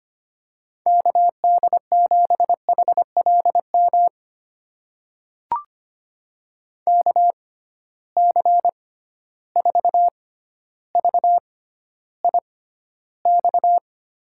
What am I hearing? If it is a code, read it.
Morse code, heard as a keyed tone, about 25 wpm, spaced as for 13 wpm, 700 Hz. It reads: KD7HLM KC4VIX
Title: USA Call Signs - 6 Characters - 6x Character Spacing - 25wpm